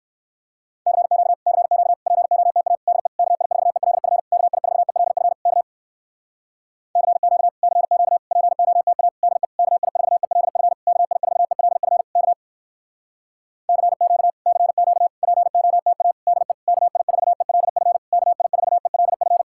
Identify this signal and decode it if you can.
Morse code, heard as a keyed tone, about 67 wpm, 700 Hz. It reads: CQ CQ POTA DE KI4ELF KI4ELF K CQ CQ POTA DE KI4ELF KI4ELF K CQ CQ POTA DE KI4ELF KI4ELF